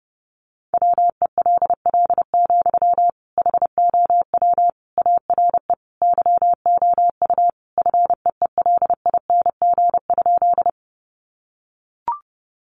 Morse code, heard as a keyed tone, about 30 wpm, 700 Hz. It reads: WELL, HOW ARE YOU FEELING?
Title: Sentences from Top 1200 Words 30wpm